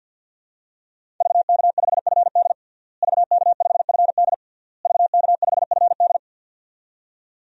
Morse code, heard as a keyed tone, about 52 wpm, 700 Hz. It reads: VK5FD VK5FD VK5FD
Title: DXCC-100 Call Signs - 5 Characters - Repeated 3x - 50wpm